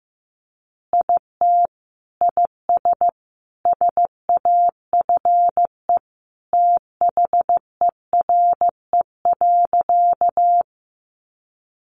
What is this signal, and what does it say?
Morse code, heard as a keyed tone, about 15 wpm, 700 Hz. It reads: IT IS SAFE THERE.
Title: Easy Phrases 15wpm